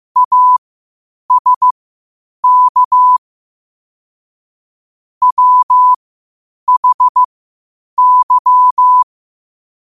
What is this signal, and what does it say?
Morse code, heard as a keyed tone, about 15 wpm, spaced as for 8 wpm, 1000 Hz. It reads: ASK WHY